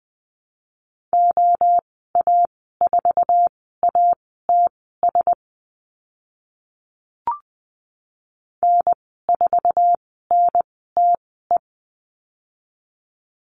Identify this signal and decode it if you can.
Morse code, heard as a keyed tone, about 20 wpm, spaced as for 14 wpm, 700 Hz. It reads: OA4ATS N4NTE
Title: 062 Lesson 11g - Letter h - Call Signs 20wpm